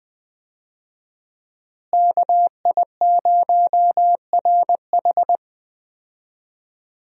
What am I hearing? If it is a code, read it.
Morse code, heard as a keyed tone, about 20 wpm, 700 Hz. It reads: KI0RH
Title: USA Call Signs - 5 Characters - Repeated 2x - 20wpm